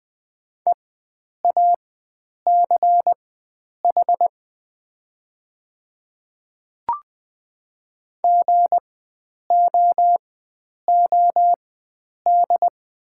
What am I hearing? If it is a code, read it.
Morse code, heard as a keyed tone, about 20 wpm, spaced as for 9 wpm, 700 Hz. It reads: EACH GOOD
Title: 4x Character Spacing - Top 500 Words - 20wpm